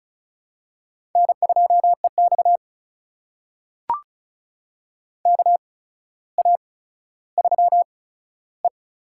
Morse code, heard as a keyed tone, about 36 wpm, spaced as for 15 wpm, 700 Hz. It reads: N2EX KA3E
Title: USA Call Signs - 4 Characters - 8x Character Spacing - 35wpm